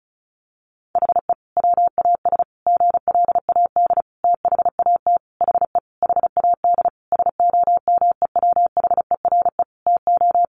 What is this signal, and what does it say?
Morse code, heard as a keyed tone, about 34 wpm, 700 Hz. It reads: HE WAS GLAD THAT HE HAD SOMEWHERE TO
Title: Sentences from Top 1400 Words 35wpm